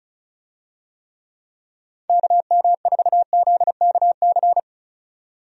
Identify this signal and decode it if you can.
Morse code, heard as a keyed tone, about 36 wpm, 700 Hz. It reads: KM4ZKC
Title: USA Call Signs - 6 Characters - 35wpm